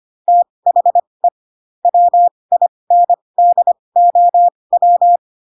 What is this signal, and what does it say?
Morse code, heard as a keyed tone, about 25 wpm, spaced as for 20 wpm, 700 Hz. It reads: THE WINDOW
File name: Pinocchio / Pinocchio ch7 15 25 ik1hge